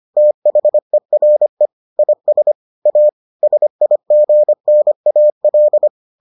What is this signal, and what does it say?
Morse code, heard as a keyed tone, about 25 wpm, 600 Hz. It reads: THERE IS A SIGNAL